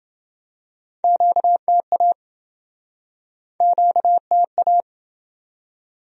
Morse code, heard as a keyed tone, about 28 wpm, 700 Hz. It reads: QTA QTA